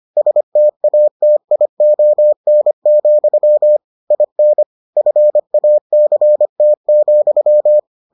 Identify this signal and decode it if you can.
Morse code, heard as a keyed tone, about 25 wpm, 600 Hz. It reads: STATION, IN FACT,